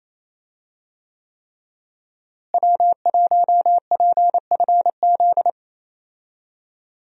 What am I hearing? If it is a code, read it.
Morse code, heard as a keyed tone, about 28 wpm, 700 Hz. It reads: W1PFZ
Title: USA Call Signs - 5 Characters - Repeated 3x - 28wpm